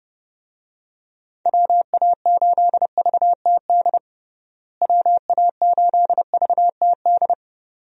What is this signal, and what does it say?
Morse code, heard as a keyed tone, about 30 wpm, 700 Hz. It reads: WA8VTD WA8VTD
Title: USA Call Signs - 6 Characters - Repeated 2x - 30wpm